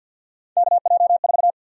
Morse code, heard as a keyed tone, about 46 wpm, 700 Hz. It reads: KJ4
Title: USA Call Sign Prefixes - 50wpm